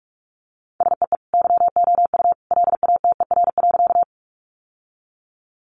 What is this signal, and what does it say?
Morse code, heard as a keyed tone, about 44 wpm, 700 Hz. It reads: SEE YOU LATER.
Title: Sentences from Top 700 Words 45wpm